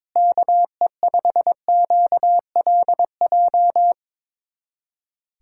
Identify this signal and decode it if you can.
Morse code, heard as a keyed tone, about 22 wpm, 700 Hz. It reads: KE5QLJ
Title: USA Callsign DE Callsign - 6 Characters - 22wpm